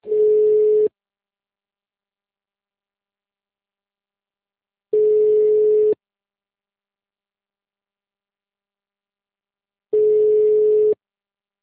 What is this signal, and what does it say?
Ringback tone (Europe, 425 Hz).